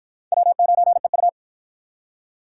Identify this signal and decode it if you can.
Morse code, heard as a keyed tone, about 52 wpm, 700 Hz. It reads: W9EU